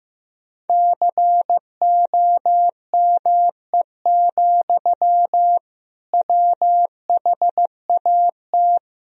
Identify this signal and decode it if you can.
Morse code, heard as a keyed tone, about 15 wpm, 700 Hz. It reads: COME, WHAT
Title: Sentences from Top 1000 Words - Review Part 3 - 15wpm